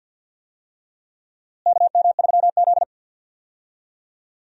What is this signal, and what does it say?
Morse code, heard as a keyed tone, about 48 wpm, 700 Hz. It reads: KM3Z
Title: USA Call Signs - 4 Characters - Repeated 2x - 50wpm